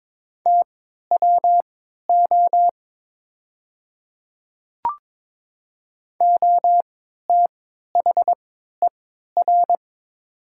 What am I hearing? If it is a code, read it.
Morse code, heard as a keyed tone, about 22 wpm, spaced as for 13 wpm, 700 Hz. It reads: TWO OTHER
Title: 3x Character Spacing - Top 100 Words - 22wpm